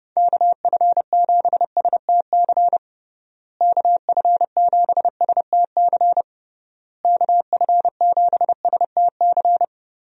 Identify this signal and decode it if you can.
Morse code, heard as a keyed tone, about 30 wpm, 700 Hz. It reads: KF7STC KF7STC KF7STC
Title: USA Call Signs - 6 Characters - Repeated 3x - 30wpm